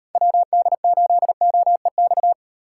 Morse code, heard as a keyed tone, about 36 wpm, 700 Hz. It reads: WD8OEX